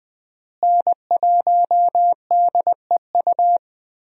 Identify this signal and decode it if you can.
Morse code, heard as a keyed tone, about 20 wpm, 700 Hz. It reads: N1DEU
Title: USA Call Signs - 5 Characters - Repeated 2x - 20wpm